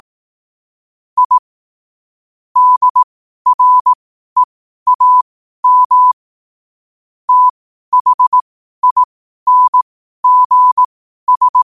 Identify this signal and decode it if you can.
Morse code, heard as a keyed tone, about 18 wpm, spaced as for 12 wpm, 1000 Hz. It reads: I DREAM THINGS
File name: qotdfiles / 12wpm